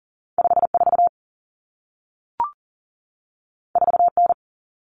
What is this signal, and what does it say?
Morse code, heard as a keyed tone, about 40 wpm, 700 Hz. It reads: 54 4N
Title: Common Cut Numbers - 2-Digit 40wpm